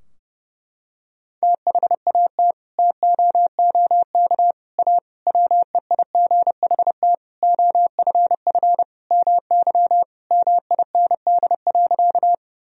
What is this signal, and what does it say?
Morse code, heard as a keyed tone, about 32 wpm, 700 Hz. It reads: THAT TOOK A WEIGHT OFF MY MIND.
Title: Sentences from Top 1000 Words - Review Part 3 - 30wpm